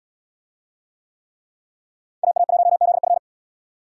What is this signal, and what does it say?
Morse code, heard as a keyed tone, about 77 wpm, 700 Hz. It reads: AA1ZF